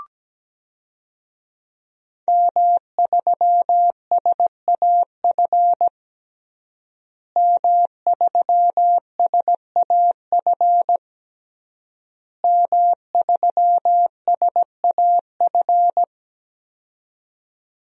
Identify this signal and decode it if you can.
Morse code, heard as a keyed tone, about 17 wpm, 700 Hz. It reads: M3SAF M3SAF M3SAF
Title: DXCC-100 Call Signs - 5 Characters - Repeated 3x - 17wpm